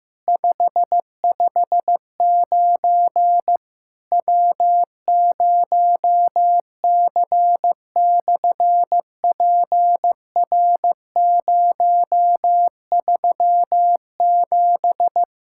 Morse code, heard as a keyed tone, about 15 wpm, 700 Hz. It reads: 559 W0C/PR037
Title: SOTA QSO Practice 15wpm